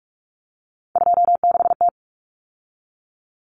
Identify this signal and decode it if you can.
Morse code, heard as a keyed tone, about 44 wpm, 700 Hz. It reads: 26T
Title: Common Cut Numbers - 3-Digit 45wpm